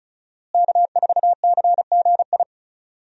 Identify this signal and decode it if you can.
Morse code, heard as a keyed tone, about 36 wpm, 700 Hz. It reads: K4CGI
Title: USA Call Signs - 5 Characters - Repeated 3x - 35wpm